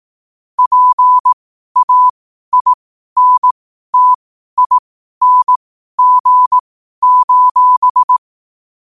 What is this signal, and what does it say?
Morse code, heard as a keyed tone, about 18 wpm, spaced as for 12 wpm, 1000 Hz. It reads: PAINTING:
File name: qotdfiles / 12wpm